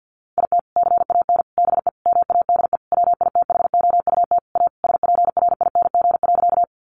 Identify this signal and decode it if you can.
Morse code, heard as a keyed tone, about 49 wpm, 700 Hz. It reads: IT CAN BE MADE WITHOUT A SPRING.
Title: Sentences from Top 1200 Words 50wpm